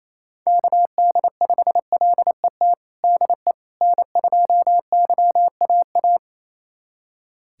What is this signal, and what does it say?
Morse code, heard as a keyed tone, about 28 wpm, 700 Hz. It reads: KD5LET DE N2YAA
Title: USA Callsign DE Callsign 28wpm